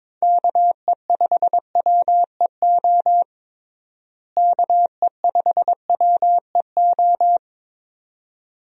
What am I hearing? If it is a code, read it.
Morse code, heard as a keyed tone, about 22 wpm, 700 Hz. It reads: KE5WEO KE5WEO
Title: USA Call Signs - 6 Characters - Repeated 2x - 22wpm